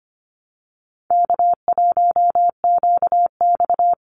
Morse code, heard as a keyed tone, about 25 wpm, 700 Hz. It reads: K1QX
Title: Easy Call Signs Repeated 2x - 25wpm